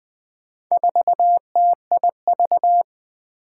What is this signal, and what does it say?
Morse code, heard as a keyed tone, about 20 wpm, 700 Hz. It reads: 4TIV
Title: Four Challenging Letter-Number Combinations 20wpm